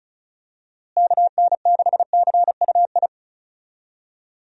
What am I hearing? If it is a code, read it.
Morse code, heard as a keyed tone, about 36 wpm, 700 Hz. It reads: KN6CUI
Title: USA Call Signs - 6 Characters - Repeated 3x - 35wpm